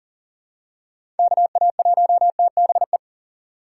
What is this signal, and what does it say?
Morse code, heard as a keyed tone, about 40 wpm, 700 Hz. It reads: KA1TBE
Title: USA Call Signs - 6 Characters - 6x Character Spacing - 40wpm